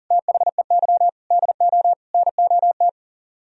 Morse code, heard as a keyed tone, about 52 wpm, 700 Hz. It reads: THEY DO NOT